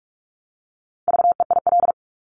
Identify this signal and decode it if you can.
Morse code, heard as a keyed tone, about 46 wpm, 700 Hz. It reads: VEIL
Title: Multi-letter Phonemes Repeated 3x 45wpm